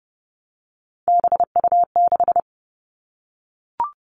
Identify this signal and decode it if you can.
Morse code, heard as a keyed tone, about 32 wpm, 700 Hz. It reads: BU6
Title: Three Challenging Letter-Number Combinations 30wpm